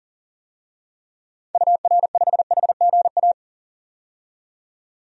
Morse code, heard as a keyed tone, about 40 wpm, 700 Hz. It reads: UR5HGA
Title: DXCC-100 Call Signs - 6 Characters - 40wpm